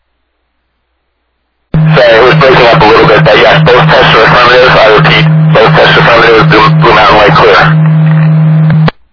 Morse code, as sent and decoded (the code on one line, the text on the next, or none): none